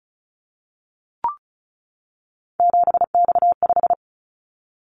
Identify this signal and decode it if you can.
Morse code, heard as a keyed tone, about 36 wpm, 700 Hz. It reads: 7X5
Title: Callsign Prefixes for Top 100 DXCC - 35wpm